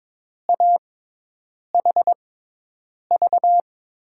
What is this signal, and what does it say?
Morse code, heard as a keyed tone, about 22 wpm, spaced as for 8 wpm, 700 Hz. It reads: AHV
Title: USA Call Signs - 6 Characters - 6x Character Spacing - 22wpm